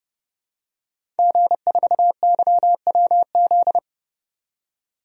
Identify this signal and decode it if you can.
Morse code, heard as a keyed tone, about 31 wpm, 700 Hz. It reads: G4YWZ
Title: DXCC-100 Call Signs - 5 Characters - 30wpm